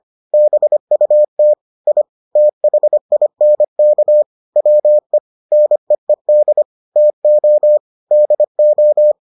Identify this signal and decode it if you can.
Morse code, heard as a keyed tone, about 25 wpm, 600 Hz. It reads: BUT I THINK WE NEED TO DO